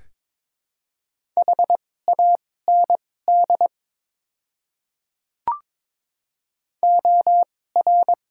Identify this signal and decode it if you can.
Morse code, heard as a keyed tone, about 22 wpm, spaced as for 16 wpm, 700 Hz. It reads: HAND OR
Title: 2x Character Spacing - Top 200 Words - 22wpm